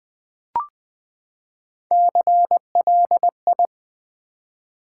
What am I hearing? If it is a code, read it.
Morse code, heard as a keyed tone, about 20 wpm, 700 Hz. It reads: CLI
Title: Build A Word - 3 Letters At A Time - 6-Letter Words - 20wpm